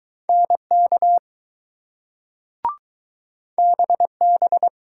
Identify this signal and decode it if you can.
Morse code, heard as a keyed tone, about 20 wpm, 700 Hz. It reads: NK BB